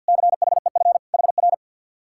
Morse code, heard as a keyed tone, about 67 wpm, spaced as for 58 wpm, 700 Hz. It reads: CHEF HR
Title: Speed-Racing Common QSO Phrases Encore - 50wpm